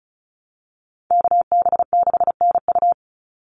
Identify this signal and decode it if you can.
Morse code, heard as a keyed tone, about 34 wpm, 700 Hz. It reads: KB6NU
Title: Call Signs Repeated 2x - 35wpm